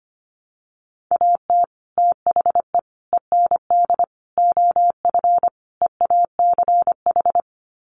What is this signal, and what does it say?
Morse code, heard as a keyed tone, about 25 wpm, 700 Hz. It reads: AT THE END OF EACH
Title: Sets of 5 Words Encore 25wpm